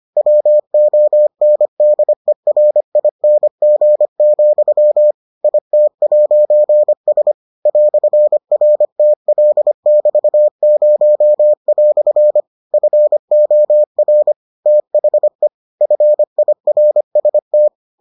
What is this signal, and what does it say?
Morse code, heard as a keyed tone, about 25 wpm, 600 Hz. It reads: WONDERING, IT'S "RTL<BT>0" FOR THE FIRST